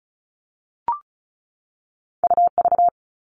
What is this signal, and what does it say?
Morse code, heard as a keyed tone, about 36 wpm, 700 Hz. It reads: UV
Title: All Cut Numbers - 2-Digit 35wpm